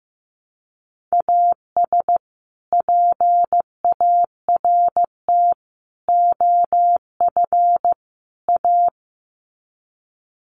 Morse code, heard as a keyed tone, about 14 wpm, 700 Hz. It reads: AS PART OF A
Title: Sets of 4 Words 15wpm